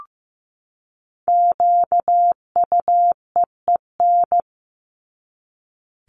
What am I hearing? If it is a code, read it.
Morse code, heard as a keyed tone, about 15 wpm, 700 Hz. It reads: QUEEN
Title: Top 1751-2000 Words 15wpm